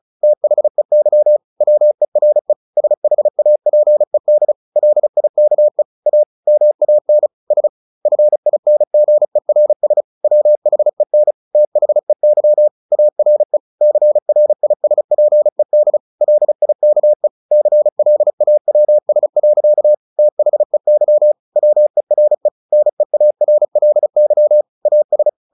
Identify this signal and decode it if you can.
Morse code, heard as a keyed tone, about 34 wpm, 600 Hz. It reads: THEY WERE SHAPED LIKE A MAN S FINGERS WHEN THEY ARE CRISPED LIKE CLAWS. THEY WERE NEARLY AS
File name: the old man and the sea 2 35wpm